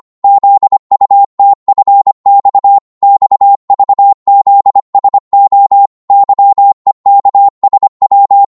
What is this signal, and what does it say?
Morse code, heard as a keyed tone, about 25 wpm, 800 Hz. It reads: ZUTFX XVZSO YEKSW